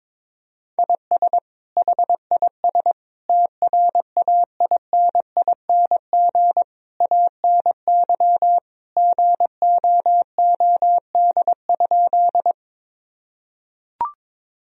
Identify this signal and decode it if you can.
Morse code, heard as a keyed tone, about 22 wpm, 700 Hz. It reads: IS HIS TRAINING ANY GOOD?
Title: Sentences from Top 1000 Words - Review Part 4 - 22wpm